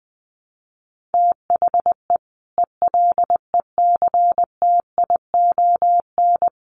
Morse code, heard as a keyed tone, about 19 wpm, 700 Hz. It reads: THE ELECTION
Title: Sentences from Top 1500 Words 20wpm